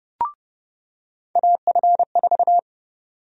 Morse code, heard as a keyed tone, about 30 wpm, 700 Hz. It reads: AF4
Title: USA Call Sign Prefixes - 30wpm